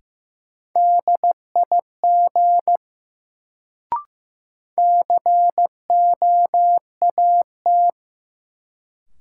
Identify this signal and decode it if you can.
Morse code, heard as a keyed tone, about 15 wpm, 700 Hz. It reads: DIG COAT
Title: Sentences from Top 1600 Words - New Word Drill Alt - 15wpm